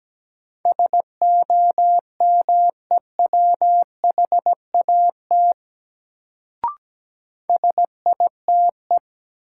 Morse code, heard as a keyed tone, about 17 wpm, 700 Hz. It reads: SOMEWHAT SITE
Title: Sentences from Top 1000 Words - Review Part 1 - New Word Drill - 17wpm